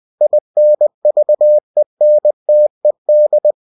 Morse code, heard as a keyed tone, about 20 wpm, 600 Hz. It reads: INVENTED
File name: cw-books / winnie-the-pooh / winnie-20wpm-page-0103